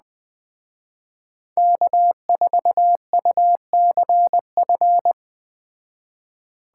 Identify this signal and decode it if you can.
Morse code, heard as a keyed tone, about 21 wpm, 700 Hz. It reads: K4UCF